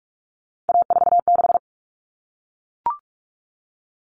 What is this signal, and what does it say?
Morse code, heard as a keyed tone, about 44 wpm, 700 Hz. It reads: A46
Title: Common Cut Numbers - 3-Digit 45wpm